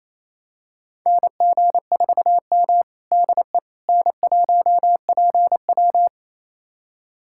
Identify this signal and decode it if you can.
Morse code, heard as a keyed tone, about 28 wpm, 700 Hz. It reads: NG4M DE N1PW
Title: USA Callsign DE Callsign - 4 Characters - 28wpm